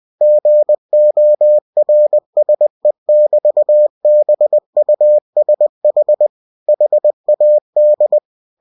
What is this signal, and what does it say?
Morse code, heard as a keyed tone, about 20 wpm, 600 Hz. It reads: GORSE<BT>BUSH HAD
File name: winnie-20wpm